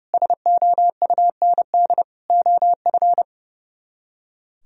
Morse code, heard as a keyed tone, about 37 wpm, 700 Hz. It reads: SOUND OF
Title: Speed-Racing - Sets of 3 Words 30wpm